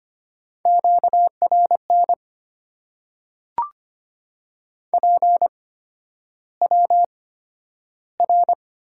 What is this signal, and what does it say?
Morse code, heard as a keyed tone, about 25 wpm, spaced as for 11 wpm, 700 Hz. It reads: QRN PWR
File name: Top 90 QSO Elements - 8x Character Spacing - 25wpm